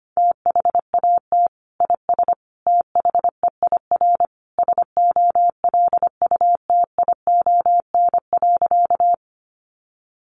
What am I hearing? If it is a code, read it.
Morse code, heard as a keyed tone, about 25 wpm, 700 Hz. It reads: THAT IS THEIR SOLUTION.
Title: Sentences from Top 1300 Words 25wpm